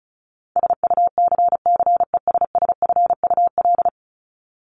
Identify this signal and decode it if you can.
Morse code, heard as a keyed tone, about 34 wpm, 700 Hz. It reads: SUCCESSFUL
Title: LY Suffix - Top 100 Words with Base Words 35wpm